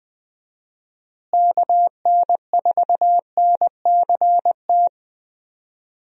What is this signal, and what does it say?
Morse code, heard as a keyed tone, about 20 wpm, 700 Hz. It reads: KN4NCT
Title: USA Call Signs - 6 Characters - Repeated 2x - 20wpm